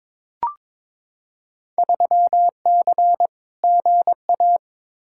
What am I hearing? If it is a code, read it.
Morse code, heard as a keyed tone, about 21 wpm, 700 Hz. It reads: 3C GA
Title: Field Day Exchanges Encore - 22wpm